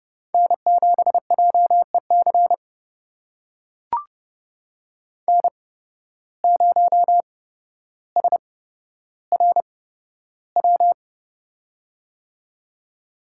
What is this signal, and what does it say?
Morse code, heard as a keyed tone, about 31 wpm, spaced as for 13 wpm, 700 Hz. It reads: N7JEC N0SRW
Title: USA Call Signs - 5 Characters - 8x Character Spacing - 30wpm